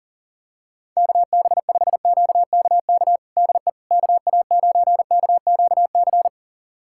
Morse code, heard as a keyed tone, about 40 wpm, 700 Hz. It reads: KB5QKK DE KA9KQC
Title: USA Callsign DE Callsign - 6 Characters - 40wpm